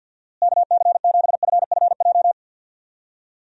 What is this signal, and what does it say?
Morse code, heard as a keyed tone, about 50 wpm, 700 Hz. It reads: KC7FFJ
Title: USA Call Signs - 6 Characters - Repeated 2x - 50wpm